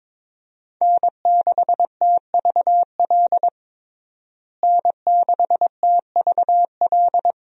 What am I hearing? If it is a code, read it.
Morse code, heard as a keyed tone, about 22 wpm, 700 Hz. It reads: N6TVL N6TVL
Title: USA Call Signs - 5 Characters - Repeated 2x - 22wpm